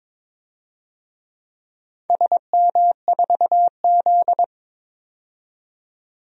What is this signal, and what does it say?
Morse code, heard as a keyed tone, about 22 wpm, 700 Hz. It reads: SM4Z
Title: DXCC-100 Call Signs - 4 Characters - 22wpm